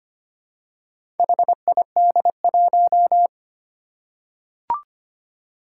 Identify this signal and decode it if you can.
Morse code, heard as a keyed tone, about 25 wpm, 700 Hz. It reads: HID1
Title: Four Challenging Letter-Number Combinations 25wpm